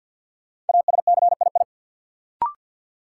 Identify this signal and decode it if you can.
Morse code, heard as a keyed tone, about 53 wpm, 700 Hz. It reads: ASCII
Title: Sets of 5 Words - New Word Drill - 50wpm